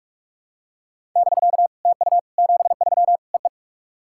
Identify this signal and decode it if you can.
Morse code, heard as a keyed tone, about 46 wpm, 700 Hz. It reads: <BK> TU 73 EE
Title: SOTA QSO Practice 45wpm